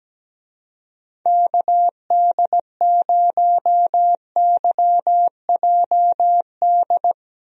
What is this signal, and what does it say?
Morse code, heard as a keyed tone, about 17 wpm, 700 Hz. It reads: KD0YJD
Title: USA Call Signs - 6 Characters - 17wpm